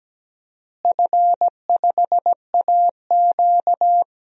Morse code, heard as a keyed tone, about 17 wpm, 700 Hz. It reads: F5AQ
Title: DXCC-100 Call Signs - 4 Characters - Repeated 3x - 17wpm